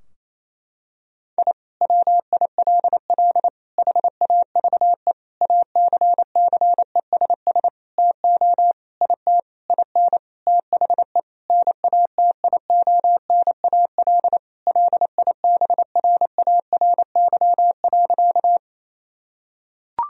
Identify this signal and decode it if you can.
Morse code, heard as a keyed tone, about 28 wpm, 700 Hz. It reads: I WILL HAVE ACCESS TO IT IN THE NATIONAL LIBRARY.
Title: Sentences from Top 1000 Words - Review Part 4 - 28wpm